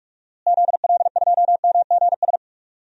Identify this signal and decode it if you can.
Morse code, heard as a keyed tone, about 46 wpm, 700 Hz. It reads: ZL2MGS
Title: DXCC-100 Call Signs - 6 Characters - Repeated 2x - 45wpm